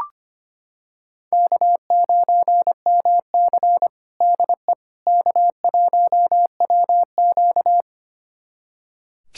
Morse code, heard as a keyed tone, about 25 wpm, 700 Hz. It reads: K9MC DE K1WQ
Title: USA Callsign DE Callsign - 4 Characters - 25wpm